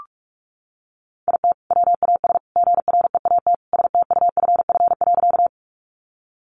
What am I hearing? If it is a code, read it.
Morse code, heard as a keyed tone, about 44 wpm, 700 Hz. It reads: IT WAS GREAT STUFF.
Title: Sentences from Top 1400 Words 45wpm